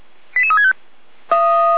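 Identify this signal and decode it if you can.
Five-tone selective call (ZVEI), heard as quick runs of five tones, tones of about 70 ms. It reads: 88365
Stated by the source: Five-tone ZVEI selective call 88352 received by SDR.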